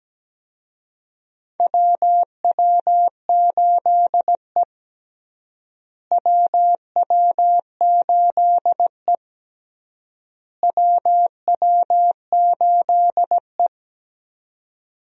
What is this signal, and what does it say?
Morse code, heard as a keyed tone, about 17 wpm, 700 Hz. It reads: WW8E WW8E WW8E
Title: USA Call Signs - 4 Characters - Repeated 3x - 17wpm